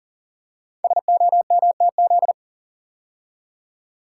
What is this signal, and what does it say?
Morse code, heard as a keyed tone, about 39 wpm, 700 Hz. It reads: SOMTZ
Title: Code Groups - 5 Letter:Numbers - 40wpm